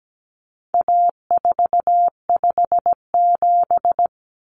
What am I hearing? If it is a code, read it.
Morse code, heard as a keyed tone, about 17 wpm, 700 Hz. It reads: A457
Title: Common Cut Numbers - 4-Digit 17wpm